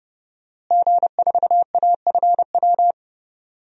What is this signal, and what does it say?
Morse code, heard as a keyed tone, about 30 wpm, 700 Hz. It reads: G4AFW
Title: DXCC-100 Call Signs - 5 Characters - Repeated 3x - 30wpm